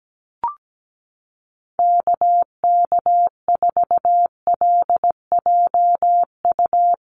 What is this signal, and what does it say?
Morse code, heard as a keyed tone, about 17 wpm, 700 Hz. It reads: KK4LJU
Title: USA Call Signs Repeated 2x 17wpm